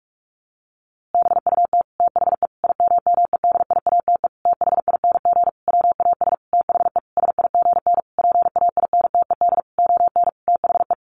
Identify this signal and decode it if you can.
Morse code, heard as a keyed tone, about 44 wpm, 700 Hz. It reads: BUT THE IMMEDIATE THING WAS THE SIGN PAINTED ON THE
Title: A Sound Of Thunder 45wpm